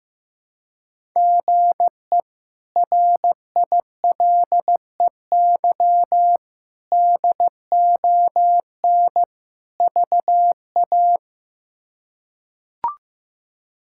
Morse code, heard as a keyed tone, about 15 wpm, 700 Hz. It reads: GE RILEY DON VA
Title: K1USN SST QSO Practice - 15wpm